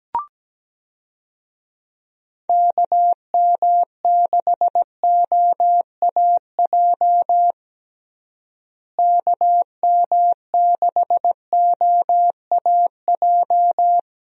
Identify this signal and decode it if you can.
Morse code, heard as a keyed tone, about 17 wpm, 700 Hz. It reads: KM6OAJ KM6OAJ
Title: USA Call Signs - 6 Characters - Repeated 3x - 17wpm